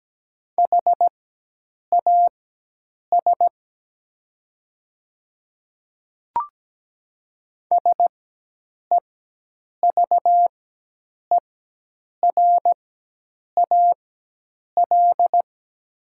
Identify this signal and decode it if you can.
Morse code, heard as a keyed tone, about 17 wpm, spaced as for 8 wpm, 700 Hz. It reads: HAS SEVERAL
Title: 4x Character Spacing - Top 500 Words - 17wpm